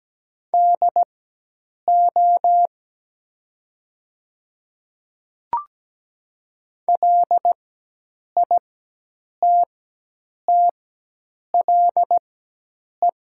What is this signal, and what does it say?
Morse code, heard as a keyed tone, about 17 wpm, spaced as for 8 wpm, 700 Hz. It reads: DO LITTLE
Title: 4x Character Spacing - Top 300 Words - 17wpm